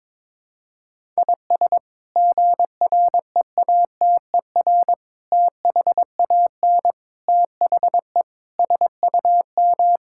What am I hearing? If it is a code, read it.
Morse code, heard as a keyed tone, about 22 wpm, 700 Hz. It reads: IS GREATER THAN THE SUM
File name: practice / 22wpm / Sets of 8 Words - 22wpm